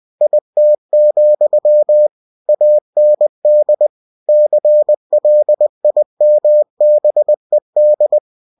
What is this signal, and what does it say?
Morse code, heard as a keyed tone, about 20 wpm, 600 Hz. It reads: IT, AND CLIMBED
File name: cw-books / winnie-the-pooh / winnie-20wpm-page-0087